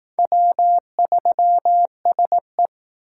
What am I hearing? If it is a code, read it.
Morse code, heard as a keyed tone, about 19 wpm, 700 Hz. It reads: W3SE